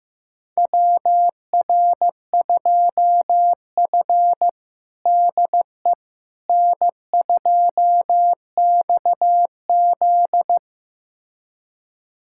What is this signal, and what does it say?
Morse code, heard as a keyed tone, about 16 wpm, 700 Hz. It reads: WR2F DE N2XZ
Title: USA Callsign DE Callsign - 4 Characters - 15wpm